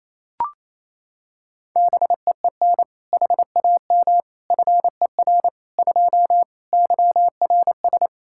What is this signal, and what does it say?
Morse code, heard as a keyed tone, about 28 wpm, 700 Hz. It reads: BEEN HAM FER 2 YRS
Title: Practice QSO - Exchange 3 - More Info - 28wpm